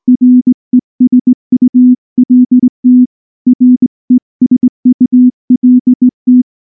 Morse code, heard as a keyed tone, about 19 wpm, 250 Hz. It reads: RESULT RESULT
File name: morse-training-20WPM